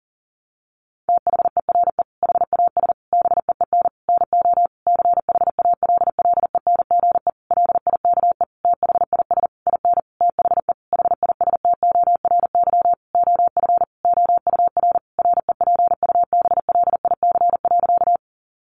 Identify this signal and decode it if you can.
Morse code, heard as a keyed tone, about 37 wpm, 700 Hz. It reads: THERE HAS BEEN NO CHALLENGE LIKE THIS IN THE HISTORY OF OUR REPUBLIC.
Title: Sentences from Top 1500 Words 40wpm